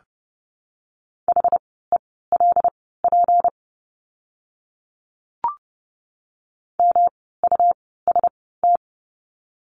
Morse code, heard as a keyed tone, about 31 wpm, spaced as for 17 wpm, 700 Hz. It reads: HELP MUST
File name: course / 30:10wpm / 237 Lesson 38f - Letter q - Words 30wpm